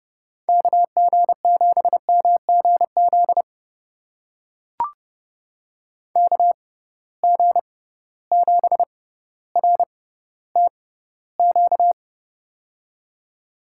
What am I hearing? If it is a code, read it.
Morse code, heard as a keyed tone, about 31 wpm, spaced as for 15 wpm, 700 Hz. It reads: KG7MGZ KG7RTQ
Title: USA Call Signs - 6x Character Spacing - 30wpm